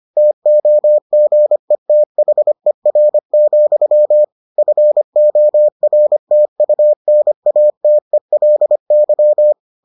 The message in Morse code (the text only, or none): TOGETHER, FORTUNATELY